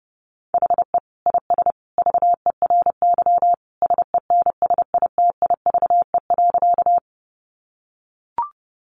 Morse code, heard as a keyed tone, about 30 wpm, 700 Hz. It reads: HE IS VERY SENSITIVE.
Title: Sentences from Top 1400 Words 30wpm